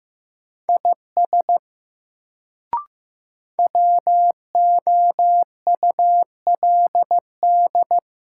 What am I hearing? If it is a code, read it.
Morse code, heard as a keyed tone, about 15 wpm, 700 Hz. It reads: IS WOULD
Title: Sentences from Top 100 Words - New Word Drill - 15wpm